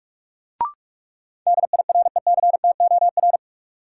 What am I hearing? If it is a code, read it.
Morse code, heard as a keyed tone, about 42 wpm, 700 Hz. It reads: DIRECTOR
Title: Sets of 3 Words - New Word Drill - 45wpm